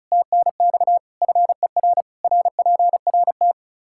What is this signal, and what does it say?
Morse code, heard as a keyed tone, about 36 wpm, 700 Hz. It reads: TNX FER RPRT
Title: Practice QSO - Exchange 2 - Info - 35wpm